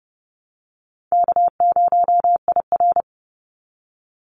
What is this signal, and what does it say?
Morse code, heard as a keyed tone, about 30 wpm, 700 Hz. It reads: K0IR
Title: Call Signs - 30wpm